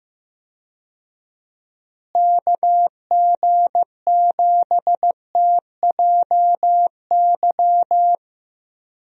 Morse code, heard as a keyed tone, about 15 wpm, 700 Hz. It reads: KG7TJY
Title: USA Call Signs - 6 Characters - Repeated 3x - 15wpm